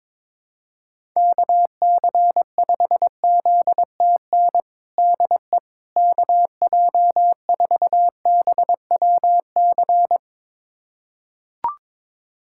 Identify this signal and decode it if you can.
Morse code, heard as a keyed tone, about 22 wpm, 700 Hz. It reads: KC5ZTN DE KJ4BWC
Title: USA Callsign DE Callsign 22wpm